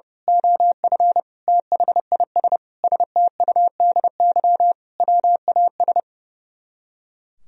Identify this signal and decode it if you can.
Morse code, heard as a keyed tone, about 28 wpm, 700 Hz. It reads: OF THIS STUDY WAS
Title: Speed-Racing - Sets of 5 Words 30wpm